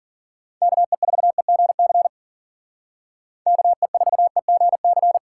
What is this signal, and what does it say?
Morse code, heard as a keyed tone, about 48 wpm, 700 Hz. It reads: KE4EGC KE4EGC
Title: Speed Racing - USA 2x3 Call Signs 40wpm